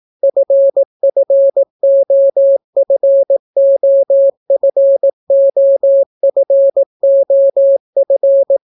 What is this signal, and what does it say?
Morse code, heard as a keyed tone, about 18 wpm, 550 Hz. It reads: FFOFOFOFOF